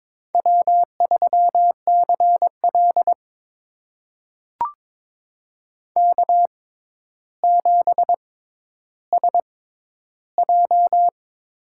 Morse code, heard as a keyed tone, about 21 wpm, spaced as for 11 wpm, 700 Hz. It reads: W3CL K7SJ
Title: USA Call Signs - 4 Characters - 6x Character Spacing - 22wpm